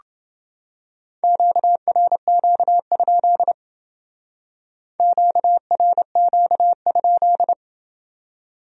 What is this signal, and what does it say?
Morse code, heard as a keyed tone, about 28 wpm, 700 Hz. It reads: QRQ? QRQ?